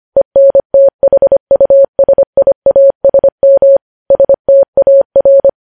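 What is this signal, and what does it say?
Morse code, heard as a keyed tone, about 25 wpm, 550 Hz. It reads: ENTHUSIASM STAR
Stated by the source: News headlines sent in Morse at 25 wpm.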